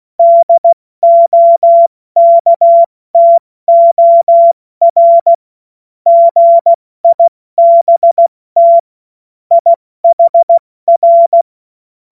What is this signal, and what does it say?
Morse code, heard as a keyed tone, about 16 wpm, 680 Hz. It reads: DOKTOR GIBT IHR